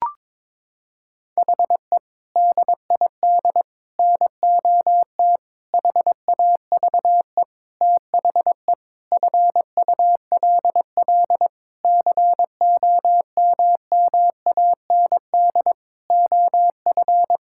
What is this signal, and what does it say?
Morse code, heard as a keyed tone, about 22 wpm, 700 Hz. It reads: HE DID NOT HAVE THE FULL COMMAND OF
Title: Sentences from Top 1600 Words 22wpm